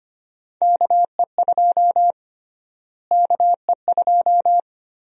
Morse code, heard as a keyed tone, about 25 wpm, 700 Hz. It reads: KE2 KE2
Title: USA Call Sign Prefixes - Repeated 2x - 25wpm